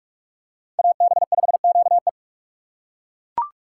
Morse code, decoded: AB5QE

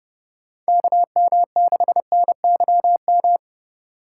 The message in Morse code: KM6NYM

